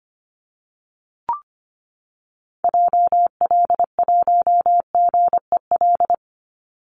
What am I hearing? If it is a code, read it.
Morse code, heard as a keyed tone, about 25 wpm, 700 Hz. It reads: JL1GEL
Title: Call Signs - 25wpm